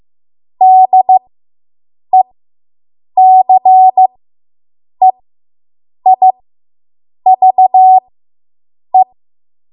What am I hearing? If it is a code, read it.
Morse code, heard as a keyed tone, about 15 wpm, spaced as for 7 wpm, 750 Hz. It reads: DECEIVE